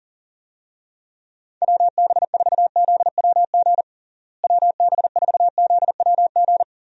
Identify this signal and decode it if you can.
Morse code, heard as a keyed tone, about 40 wpm, 700 Hz. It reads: WB4ZWG WB4ZWG
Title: USA Call Signs - 6 Characters - Repeated 3x - 40wpm